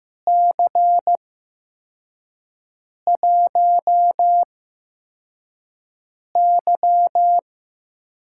Morse code, heard as a keyed tone, about 15 wpm, spaced as for 4 wpm, 700 Hz. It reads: C1Y
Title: USA Call Signs - 4 Characters - 8x Character Spacing - 15wpm